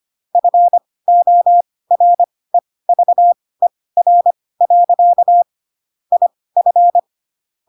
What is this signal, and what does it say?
Morse code, heard as a keyed tone, about 25 wpm, spaced as for 18 wpm, 700 Hz. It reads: FOREVER. IF
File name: Pinocchio / Pinocchio ch4 12 25 ik1hge